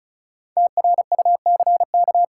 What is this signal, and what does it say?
Morse code, heard as a keyed tone, about 36 wpm, 700 Hz. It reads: TRUCK
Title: K1USN SST QSO Practice 35wpm